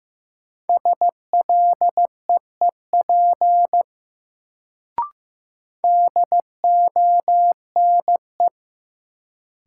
Morse code, heard as a keyed tone, about 15 wpm, 700 Hz. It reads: SLEEP DONE
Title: Sentences from Top 500 Words - New Word Drill - 15wpm